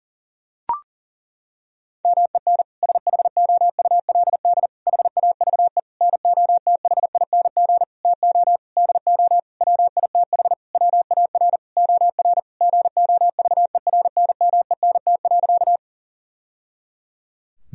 Morse code, heard as a keyed tone, about 40 wpm, 700 Hz. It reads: MEN SHOULD HAVE NOTHING TO DO WITH WAR OR GOVERNMENT.
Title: Sentences from Top 1000 Words - Review Part 3 - 40wpm